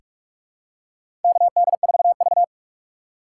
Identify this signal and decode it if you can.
Morse code, heard as a keyed tone, about 48 wpm, 700 Hz. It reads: KD4V
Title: USA Call Signs - 4 Characters - 6x Character Spacing - 45wpm